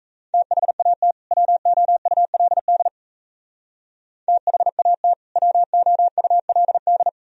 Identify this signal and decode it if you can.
Morse code, heard as a keyed tone, about 35 wpm, 700 Hz. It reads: THAT WOULD THAT WOULD
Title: Speed-Racing - Sets of 2 Words 28wpm